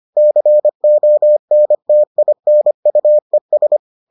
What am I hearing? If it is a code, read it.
Morse code, heard as a keyed tone, about 25 wpm, 600 Hz. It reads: CONTINUES